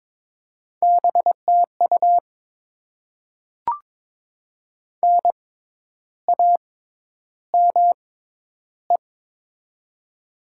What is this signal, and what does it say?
Morse code, heard as a keyed tone, about 21 wpm, spaced as for 10 wpm, 700 Hz. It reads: BTU NAME
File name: Top 90 QSO Elements - 6x Character Spacing - 22wpm